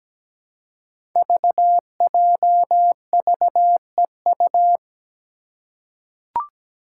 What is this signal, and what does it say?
Morse code, heard as a keyed tone, about 17 wpm, 700 Hz. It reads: VJVEU
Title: Five Challenging Letter-Number Combinations 17wpm